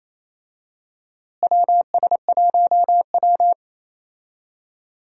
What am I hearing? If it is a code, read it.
Morse code, heard as a keyed tone, about 28 wpm, 700 Hz. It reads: WS1W
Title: USA Call Signs - 4 Characters - Repeated 2x - 28wpm